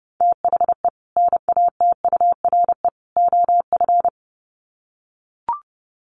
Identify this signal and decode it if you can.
Morse code, heard as a keyed tone, about 30 wpm, 700 Hz. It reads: THE NATURE OF
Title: Sets of 3 Words 30wpm